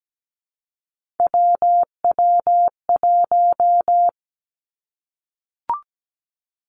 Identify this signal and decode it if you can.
Morse code, heard as a keyed tone, about 17 wpm, 700 Hz. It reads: WW1
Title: Three Challenging Letter-Number Combinations 17wpm